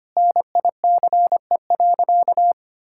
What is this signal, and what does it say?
Morse code, heard as a keyed tone, about 25 wpm, 700 Hz. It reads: NICE.
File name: practice / 25wpm / Sentences with 2-Word Phrases - 25wpm